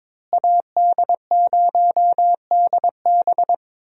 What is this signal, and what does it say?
Morse code, heard as a keyed tone, about 22 wpm, 700 Hz. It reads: AD0DB
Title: USA Call Signs - 5 Characters - Repeated 2x - 22wpm